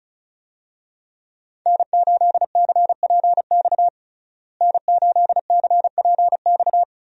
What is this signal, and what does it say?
Morse code, heard as a keyed tone, about 36 wpm, 700 Hz. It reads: N8CPX N8CPX
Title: USA Call Signs - 5 Characters - Repeated 3x - 35wpm